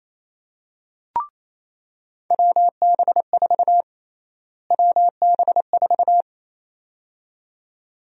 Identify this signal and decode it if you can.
Morse code, heard as a keyed tone, about 27 wpm, 700 Hz. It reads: WB4 WB4
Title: USA Call Sign Prefixes - Repeated 2x - 28wpm